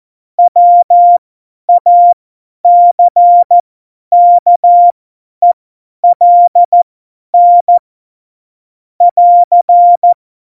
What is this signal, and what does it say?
Morse code, heard as a keyed tone, about 14 wpm, spaced as for 10 wpm, 700 Hz. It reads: WACKELN <AR>
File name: Morsekurs / Witze / Witz 14 Juli 2020, 50 F70